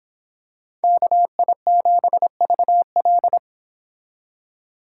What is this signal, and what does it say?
Morse code, heard as a keyed tone, about 25 wpm, 700 Hz. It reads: KI7VL